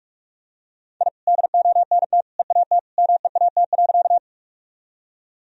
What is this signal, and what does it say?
Morse code, heard as a keyed tone, about 44 wpm, 700 Hz. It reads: I DONT EAT MEAT.
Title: Sentences from Top 1000 Words - Review Part 3 - 45wpm